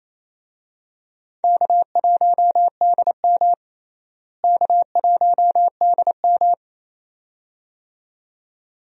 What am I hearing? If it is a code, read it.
Morse code, heard as a keyed tone, about 28 wpm, 700 Hz. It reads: K1DM K1DM
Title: USA Call Signs - 4 Characters - Repeated 2x - 28wpm